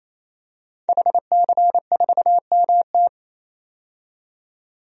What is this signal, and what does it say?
Morse code, heard as a keyed tone, about 28 wpm, 700 Hz. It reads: HC4MT
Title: DXCC-100 Call Signs - 5 Characters - 28wpm